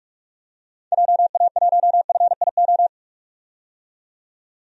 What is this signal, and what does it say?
Morse code, heard as a keyed tone, about 47 wpm, 700 Hz. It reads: JA1FIO
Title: DXCC-100 Call Signs - 6 Characters - 45wpm